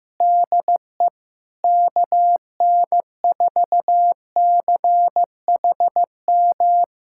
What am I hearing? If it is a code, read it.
Morse code, heard as a keyed tone, about 15 wpm, 700 Hz. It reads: DE KN4CHM
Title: USA Callsign DE Callsign - 6 Characters - 15wpm